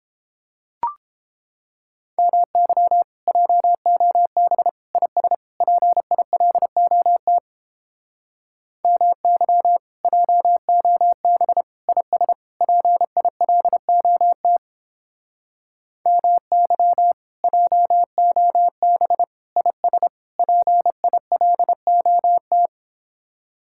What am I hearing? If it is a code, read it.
Morse code, heard as a keyed tone, about 31 wpm, 700 Hz. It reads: MY JOB IS PILOT MY JOB IS PILOT MY JOB IS PILOT